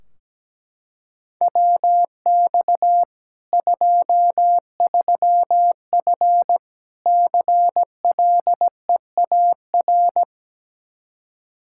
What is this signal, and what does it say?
Morse code, heard as a keyed tone, about 17 wpm, 700 Hz. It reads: WX 23F CLEAR
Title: K1USN SST QSO Practice 17wpm